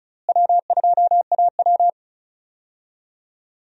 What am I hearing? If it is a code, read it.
Morse code, heard as a keyed tone, about 36 wpm, 700 Hz. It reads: W2AW
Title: USA Call Signs - 4 Characters - Repeated 3x - 35wpm